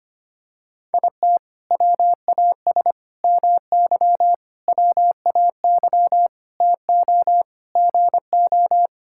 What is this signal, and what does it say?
Morse code, heard as a keyed tone, about 25 wpm, 700 Hz. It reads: IT WAS MY WAY TO GO
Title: Sentences from Top 1600 Words 25wpm